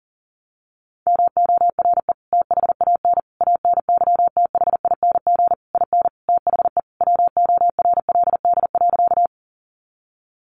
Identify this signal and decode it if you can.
Morse code, heard as a keyed tone, about 37 wpm, 700 Hz. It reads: MORE THAN ANYTHING IN THE WORLD.
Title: Sentences from Top 1000 Words 40wpm